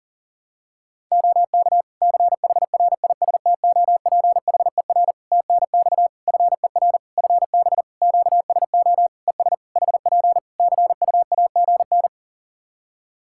Morse code, heard as a keyed tone, about 40 wpm, 700 Hz. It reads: OK CHRISTOPHER TNX FER FB QSO ES HP CUAGN
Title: Practice QSO - Exchange 4 - Ending - 40wpm